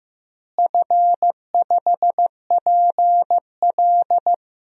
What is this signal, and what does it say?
Morse code, heard as a keyed tone, about 15 wpm, 700 Hz. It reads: F5PL
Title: DXCC-100 Call Signs - 4 Characters - Repeated 2x - 15wpm